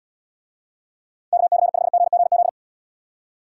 Morse code, heard as a keyed tone, about 73 wpm, 700 Hz. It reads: KC5GGZ